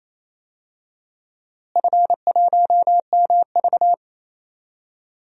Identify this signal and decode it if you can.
Morse code, heard as a keyed tone, about 28 wpm, 700 Hz. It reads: F1MV
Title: DXCC-100 Call Signs - 4 Characters - Repeated 2x - 28wpm